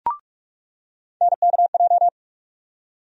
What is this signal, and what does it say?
Morse code, heard as a keyed tone, about 42 wpm, 700 Hz. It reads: NKJ
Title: USA Call Sign Postfixes - 3 Letters 45wpm